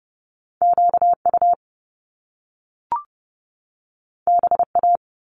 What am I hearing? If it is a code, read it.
Morse code, heard as a keyed tone, about 30 wpm, 700 Hz. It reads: QU BA
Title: Top Two Letter Combinations in Words 30wpm